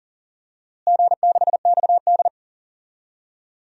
Morse code, heard as a keyed tone, about 37 wpm, 700 Hz. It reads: G6XD
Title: DXCC-100 Call Signs - 4 Characters - Repeated 3x - 40wpm